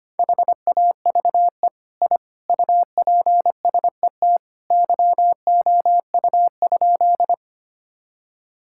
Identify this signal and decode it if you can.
Morse code, heard as a keyed tone, about 25 wpm, 700 Hz. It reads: HAVE I UPSET YOU?
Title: Sentences from Top 1000 Words - Review Part 5 - 25wpm